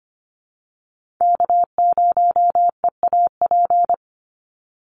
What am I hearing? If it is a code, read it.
Morse code, heard as a keyed tone, about 25 wpm, 700 Hz. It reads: K0EAP
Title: NA SOTA Chaser Callsigns 25wpm